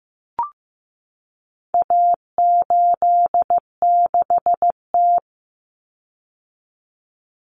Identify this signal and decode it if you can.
Morse code, heard as a keyed tone, about 15 wpm, 700 Hz. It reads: A86T
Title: Common Cut Numbers - 4-Digit 15wpm